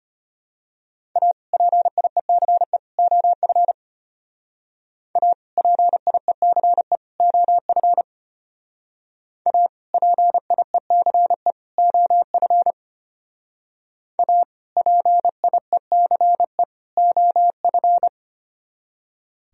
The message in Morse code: A PIECE OF A PIECE OF A PIECE OF A PIECE OF

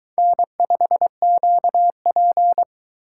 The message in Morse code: N5QP

